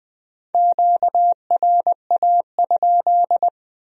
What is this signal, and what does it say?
Morse code, heard as a keyed tone, about 21 wpm, 700 Hz. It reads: QRA?